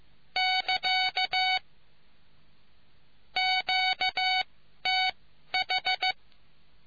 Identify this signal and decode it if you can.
Morse code, heard as a keyed tone, about 15 wpm, spaced as for 11 wpm, 750 Hz. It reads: <KA> QTH